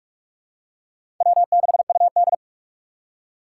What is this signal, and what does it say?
Morse code, heard as a keyed tone, about 46 wpm, 700 Hz. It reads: W6UD
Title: USA Call Signs - 4 Characters - 4x Character Spacing - 45wpm